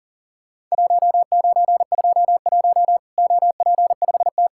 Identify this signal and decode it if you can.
Morse code, heard as a keyed tone, about 40 wpm, 700 Hz. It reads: 1921 OP5T
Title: Worked All Europe DX Contest - QTC Exchange - 40wpm